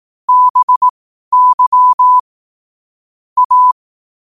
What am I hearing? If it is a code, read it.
Morse code, heard as a keyed tone, about 18 wpm, spaced as for 12 wpm, 1000 Hz. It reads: BY A
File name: qotdfiles / 12wpm